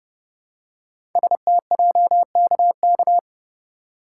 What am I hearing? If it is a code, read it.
Morse code, heard as a keyed tone, about 28 wpm, 700 Hz. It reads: STJKK
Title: Code Groups - 5 Letters - 30wpm